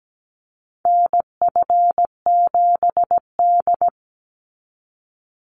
Morse code, heard as a keyed tone, about 17 wpm, 700 Hz. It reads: NF7D
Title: Call Signs - 17wpm